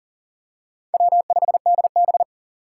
Morse code, heard as a keyed tone, about 37 wpm, 700 Hz. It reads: W5DB
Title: USA Call Signs - 6x Character Spacing - 40wpm